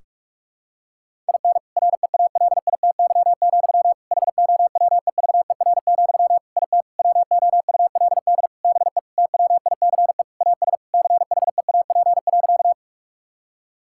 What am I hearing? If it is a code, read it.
Morse code, heard as a keyed tone, about 46 wpm, 700 Hz. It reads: IN REALITY, HOWEVER, IT WOULD BE TWICE AS CHEAP.
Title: Sentences from Top 1600 Words 45wpm